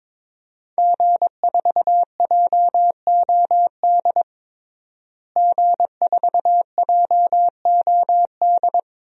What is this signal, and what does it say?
Morse code, heard as a keyed tone, about 22 wpm, 700 Hz. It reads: G4JOD G4JOD
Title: DXCC-100 Call Signs - 5 Characters - Repeated 3x - 22wpm